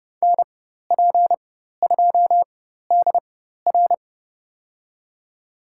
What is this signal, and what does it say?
Morse code, heard as a keyed tone, about 30 wpm, spaced as for 14 wpm, 700 Hz. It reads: NP2DR
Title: USA Call Signs - 5 Characters - 4x Character Spacing - 30wpm